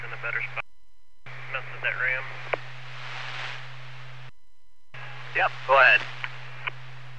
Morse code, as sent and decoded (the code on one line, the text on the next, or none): none